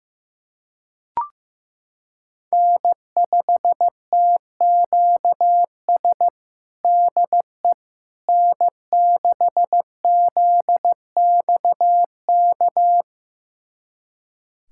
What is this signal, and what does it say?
Morse code, heard as a keyed tone, about 15 wpm, 700 Hz. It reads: N5TQS DE N6ZXK
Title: USA Callsign DE Callsign - 5 Characters - 15wpm